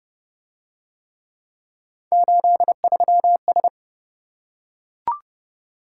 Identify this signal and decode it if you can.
Morse code, heard as a keyed tone, about 28 wpm, 700 Hz. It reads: 83S